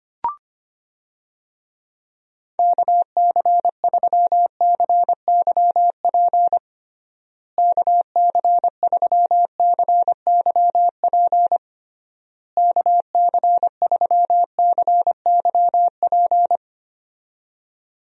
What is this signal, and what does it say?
Morse code, heard as a keyed tone, about 25 wpm, 700 Hz. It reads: KC3CYP KC3CYP KC3CYP
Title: USA Call Signs - 6 Characters - Repeated 3x - 25wpm